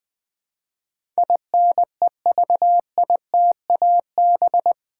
Morse code, heard as a keyed tone, about 20 wpm, 700 Hz. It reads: INEVITAB
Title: Build A Word - 4 Letters At A Time - 10-Letter Words - 20wpm